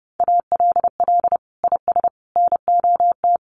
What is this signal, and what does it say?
Morse code, heard as a keyed tone, about 28 wpm, 700 Hz. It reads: ALL IS NOT
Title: Sentences from Top 600 Words 30wpm